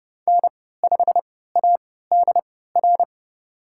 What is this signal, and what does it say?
Morse code, heard as a keyed tone, about 30 wpm, spaced as for 17 wpm, 700 Hz. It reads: N5ADR
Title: USA Call Signs - 5 Characters - 3x Character Spacing - 30wpm